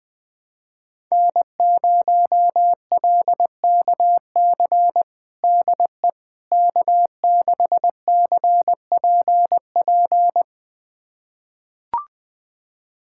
Morse code, heard as a keyed tone, about 21 wpm, 700 Hz. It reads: N0LKC DE K6CPP
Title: USA Callsign DE Callsign - 5 Characters - 20wpm